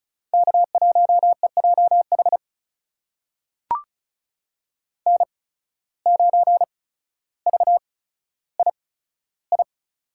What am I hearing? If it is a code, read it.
Morse code, heard as a keyed tone, about 36 wpm, spaced as for 15 wpm, 700 Hz. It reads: K1EJH N9VII
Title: USA Call Signs - 5 Characters - 8x Character Spacing - 35wpm